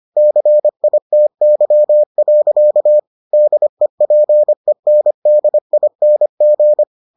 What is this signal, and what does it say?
Morse code, heard as a keyed tone, about 25 wpm, 600 Hz. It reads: CITY. DEPENDING